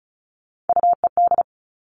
Morse code, heard as a keyed tone, about 36 wpm, 700 Hz. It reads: UED
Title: Three Challenging Letter-Number Combinations 35wpm